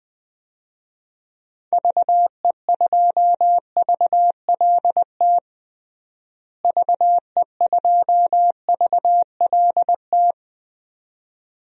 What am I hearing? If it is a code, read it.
Morse code, heard as a keyed tone, about 20 wpm, 700 Hz. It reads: VE2VLT VE2VLT
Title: DXCC-100 Call Signs - 6 Characters - Repeated 2x - 20wpm